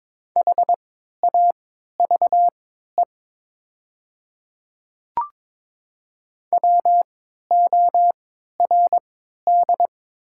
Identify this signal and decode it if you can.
Morse code, heard as a keyed tone, about 22 wpm, spaced as for 12 wpm, 700 Hz. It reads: HAVE WORD
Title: 3x Character Spacing - Top 100 Words - 22wpm